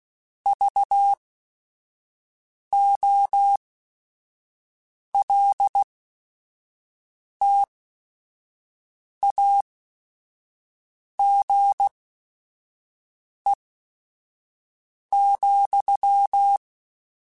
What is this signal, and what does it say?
Morse code, heard as a keyed tone, about 16 wpm, spaced as for 5 wpm, 800 Hz. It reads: VOLTAGE,
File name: Technical Subjects / Technical Subjects-mp3s / Sw Pwr Sup @5